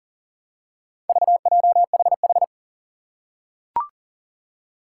Morse code, heard as a keyed tone, about 40 wpm, 700 Hz. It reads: VJHH
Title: Four Challenging Letter-Number Combinations 40wpm